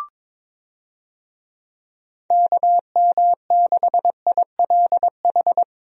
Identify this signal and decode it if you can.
Morse code, heard as a keyed tone, about 22 wpm, 700 Hz. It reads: KM6ILH
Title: USA Call Signs - 6 Characters - Repeated 2x - 22wpm